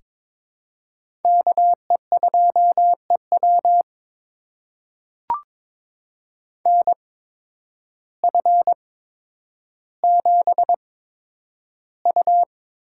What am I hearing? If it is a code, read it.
Morse code, heard as a keyed tone, about 22 wpm, spaced as for 10 wpm, 700 Hz. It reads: KE2EW NF7U
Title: USA Call Signs - 8x Character Spacing - 22wpm